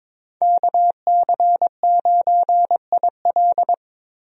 Morse code, heard as a keyed tone, about 22 wpm, 700 Hz. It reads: KC9IL
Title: USA Call Signs - 5 Characters - Repeated 2x - 22wpm